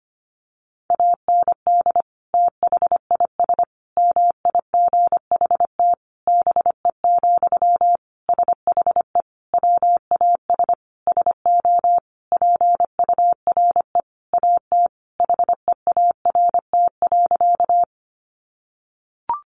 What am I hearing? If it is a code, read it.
Morse code, heard as a keyed tone, about 25 wpm, 700 Hz. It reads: AND THIS MIGHT BE, SHE WAS SO PURE AT HEART.
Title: Sentences from Top 1500 Words 25wpm